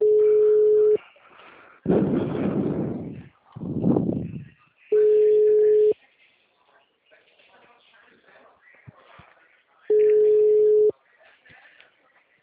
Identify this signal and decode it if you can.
Ringback tone (Europe, 425 Hz), with speech.